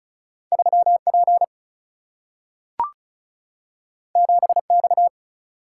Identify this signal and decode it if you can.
Morse code, heard as a keyed tone, about 36 wpm, 700 Hz. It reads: 3P 7X
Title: Single-Letter Single-Number 35wpm